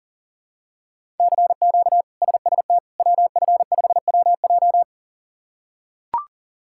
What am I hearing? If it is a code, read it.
Morse code, heard as a keyed tone, about 40 wpm, 700 Hz. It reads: CQ SST WF5WJ
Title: K1USN SST QSO Practice - 40wpm